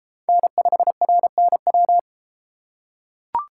Morse code, heard as a keyed tone, about 29 wpm, 700 Hz. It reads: N5RNW